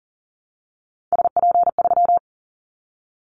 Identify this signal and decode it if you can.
Morse code, heard as a keyed tone, about 38 wpm, 700 Hz. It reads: SP3
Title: Callsign Prefixes for Top 25 DXCC - 40wpm